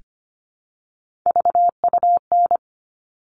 Morse code, heard as a keyed tone, about 25 wpm, 700 Hz. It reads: VUN
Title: All Cut Numbers - 3-Digit 25wpm